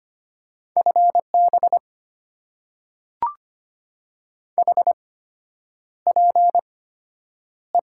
Morse code, heard as a keyed tone, about 25 wpm, spaced as for 9 wpm, 700 Hz. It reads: FB HPE
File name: Top 90 QSO Elements - 8x Character Spacing - 25wpm